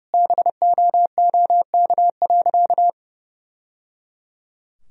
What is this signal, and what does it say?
Morse code, heard as a keyed tone, about 30 wpm, 700 Hz. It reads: BOOK.
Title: Sentences from Top 1000 Words - Review Part 4 - 30wpm